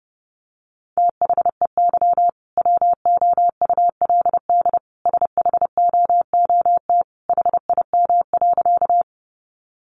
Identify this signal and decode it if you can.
Morse code, heard as a keyed tone, about 30 wpm, 700 Hz. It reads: THEY WOULD SHOOT HIM.
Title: Sentences from Top 1200 Words 30wpm